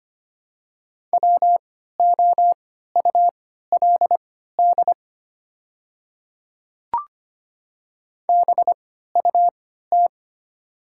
Morse code, heard as a keyed tone, about 25 wpm, spaced as for 14 wpm, 700 Hz. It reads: WOULD BUT
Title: 3x Character Spacing - Top 100 Words - 25wpm